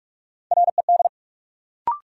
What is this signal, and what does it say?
Morse code, heard as a keyed tone, about 46 wpm, 700 Hz. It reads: RED
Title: Sentences from Top 700 Words - New Word Drill - 45wpm